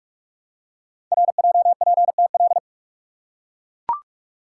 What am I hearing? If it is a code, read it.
Morse code, heard as a keyed tone, about 44 wpm, 700 Hz. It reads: RJPTL
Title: Code Groups - 5 Letters - 45wpm